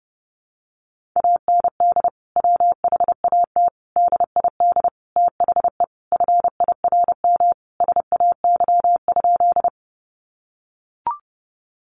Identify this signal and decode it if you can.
Morse code, heard as a keyed tone, about 30 wpm, 700 Hz. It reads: AND WHAT DID THE FIRM SAY?
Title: Sentences from Top 1400 Words 30wpm